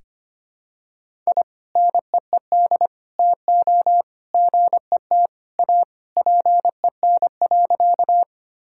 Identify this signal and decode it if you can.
Morse code, heard as a keyed tone, about 25 wpm, 700 Hz. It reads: I NEED TO GET A PEN.
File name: practice / 25wpm / Sentences with 2-Word Phrases - 25wpm